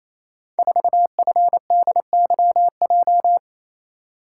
Morse code, heard as a keyed tone, about 29 wpm, 700 Hz. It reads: 4FDYJ
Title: Code Groups - 5 Letter:Numbers - 28wpm